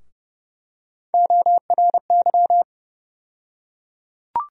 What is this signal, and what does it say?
Morse code, heard as a keyed tone, about 32 wpm, 700 Hz. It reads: ORY
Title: USA Call Sign Postfixes - 3 Letters 30wpm